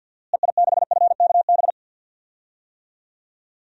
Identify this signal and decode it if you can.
Morse code, heard as a keyed tone, about 47 wpm, 700 Hz. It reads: EI6FKB
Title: DXCC-100 Call Signs - 6 Characters - Repeated 2x - 50wpm